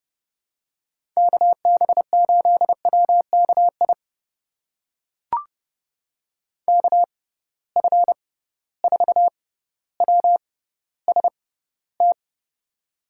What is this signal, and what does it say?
Morse code, heard as a keyed tone, about 30 wpm, spaced as for 15 wpm, 700 Hz. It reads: KB8WKI KF4WST
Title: USA Call Signs - 6 Characters - 6x Character Spacing - 30wpm